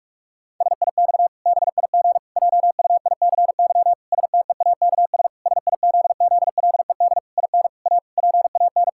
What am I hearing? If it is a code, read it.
Morse code, heard as a keyed tone, about 41 wpm, 700 Hz. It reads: SIX BIG JUICY STEAKS SIZZLED IN A PAN
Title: Pangram Challenge v2 - 45wpm